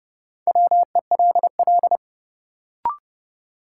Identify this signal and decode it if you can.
Morse code, heard as a keyed tone, about 30 wpm, 700 Hz. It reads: WELL
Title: Sets of 2 Words - New Word Drill - 30wpm